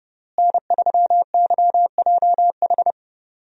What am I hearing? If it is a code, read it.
Morse code, heard as a keyed tone, about 30 wpm, 700 Hz. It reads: N3YJH
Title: USA Call Signs - 5 Characters - Repeated 3x - 30wpm